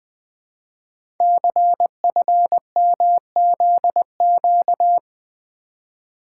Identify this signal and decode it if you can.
Morse code, heard as a keyed tone, about 20 wpm, 700 Hz. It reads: CFMZQ
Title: Code Groups - 5 Letters - 20wpm